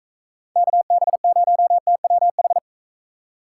Morse code, heard as a keyed tone, about 42 wpm, 700 Hz. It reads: KB0TWH